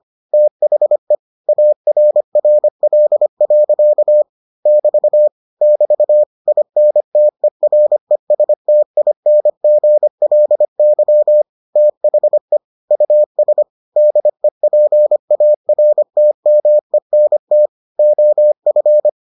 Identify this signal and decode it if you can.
Morse code, heard as a keyed tone, about 25 wpm, 600 Hz. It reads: THE ARRL. <BT> <BT> INTERESTINGLY THE US DEPARTMENT OF